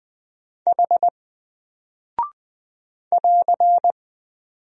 Morse code, heard as a keyed tone, about 20 wpm, 700 Hz. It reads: H <AR>
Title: Single Character (Letter:Number:Punctuation:Prosign) 20wpm